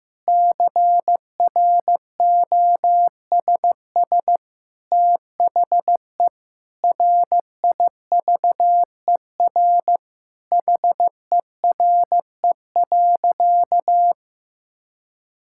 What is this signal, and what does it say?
Morse code, heard as a keyed tone, about 15 wpm, 700 Hz. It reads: CROSS THE RIVER HERE.
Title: Sentences from Top 1000 Words - Review Part 5 - 15wpm